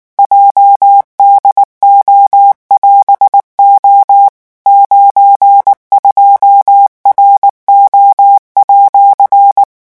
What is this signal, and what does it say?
Morse code, heard as a keyed tone, about 19 wpm, 800 Hz. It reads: JDO<AS>O 92RO@
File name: vitesse 18 mots